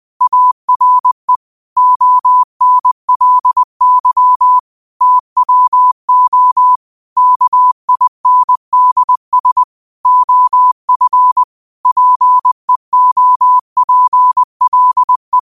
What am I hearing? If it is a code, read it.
Morse code, heard as a keyed tone, about 20 wpm, 1000 Hz. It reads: ARE ONLY TWO KINDS OF PEOPLE